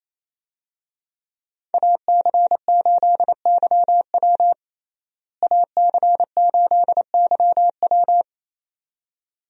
Morse code, heard as a keyed tone, about 28 wpm, 700 Hz. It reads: AC8YW AC8YW
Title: USA Call Signs - 5 Characters - Repeated 2x - 28wpm